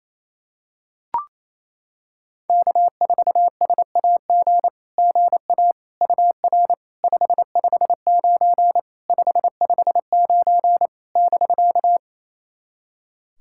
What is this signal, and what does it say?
Morse code, heard as a keyed tone, about 27 wpm, 690 Hz. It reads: K4SAG GA UR 559 559 <BK>
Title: SOTA QSO Practice 28wpm